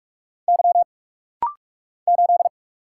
Morse code, heard as a keyed tone, about 44 wpm, 690 Hz. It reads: Y 8
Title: Single Letter-Number - Rapid-Fire 45wpm